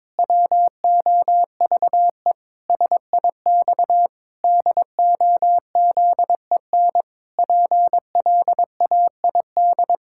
Morse code, heard as a keyed tone, about 22 wpm, 700 Hz. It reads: WOVE SIX DOZEN PLAID
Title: Pangram Challenge v2 - 22wpm